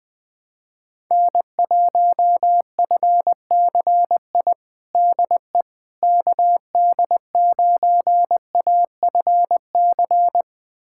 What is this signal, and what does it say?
Morse code, heard as a keyed tone, about 21 wpm, 700 Hz. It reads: N1FCI DE KD9AFC
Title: USA Callsign DE Callsign 20wpm